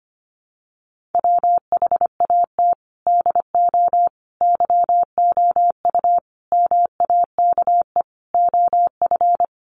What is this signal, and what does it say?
Morse code, heard as a keyed tone, about 25 wpm, 700 Hz. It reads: WHAT DO YOU MAKE OF
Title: Sentences from Top 100 Words 25wpm